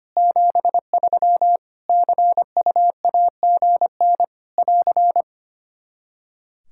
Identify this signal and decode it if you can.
Morse code, heard as a keyed tone, about 25 wpm, 700 Hz. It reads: 73 CUAGN <AR>
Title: Practice QSO - Exchange 4 - Ending - 25wpm